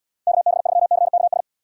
Morse code, heard as a keyed tone, about 78 wpm, 700 Hz. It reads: KB3CKS